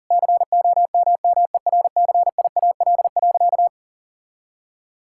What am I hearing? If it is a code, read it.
Morse code, heard as a keyed tone, about 40 wpm, 700 Hz. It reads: COMMERCIAL.
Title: Sentences from Top 1000 Words - Review Part 5 - 40wpm